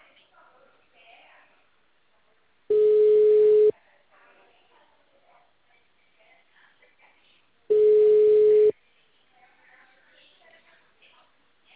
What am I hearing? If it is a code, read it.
Ringback tone (Europe, 425 Hz), with speech.
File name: out- -2012- - - .23